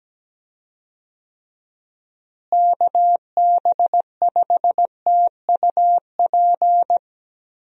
Morse code, heard as a keyed tone, about 17 wpm, 700 Hz. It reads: KB5TUP
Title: USA Call Signs - 6 Characters - Repeated 2x - 17wpm